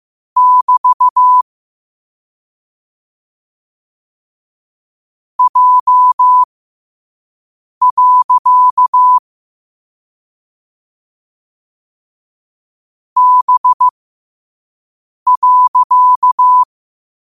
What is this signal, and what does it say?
Morse code, heard as a keyed tone, about 15 wpm, spaced as for 5 wpm, 1000 Hz. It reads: <BT> J. B.
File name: qotdfiles / 5wpm